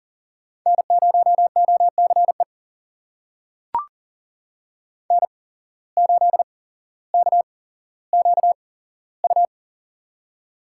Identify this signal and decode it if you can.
Morse code, heard as a keyed tone, about 39 wpm, spaced as for 17 wpm, 700 Hz. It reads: N0OCE N8KQU
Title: USA Call Signs - 8x Character Spacing - 40wpm